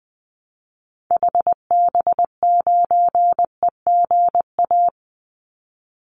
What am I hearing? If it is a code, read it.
Morse code, heard as a keyed tone, about 20 wpm, 700 Hz. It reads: HB9EGA
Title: Call Signs - 20wpm